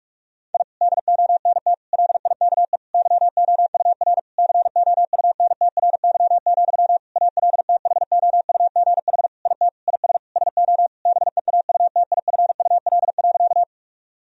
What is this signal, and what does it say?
Morse code, heard as a keyed tone, about 46 wpm, 700 Hz. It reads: I DONT LIKE YOUR COUNTRY, ALTHOUGH IT IS SO BEAUTIFUL.
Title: Sentences from Top 1000 Words - Review Part 3 - 45wpm